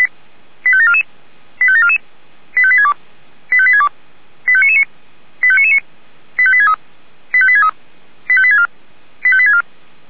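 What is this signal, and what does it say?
Five-tone selective call (ZVEI), heard as quick runs of five tones, tones of about 70 ms. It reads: 85744 85744 85862 85862 85999 85999 85863 85863 85864 85864
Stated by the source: ZVEI five-tone selective call received on SDR.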